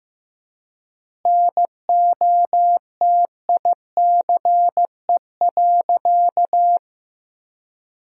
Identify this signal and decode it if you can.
Morse code, heard as a keyed tone, about 15 wpm, 700 Hz. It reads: NOTICE.
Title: 4x Word-Spacing - Sentences from Top 400 Words - 15wpm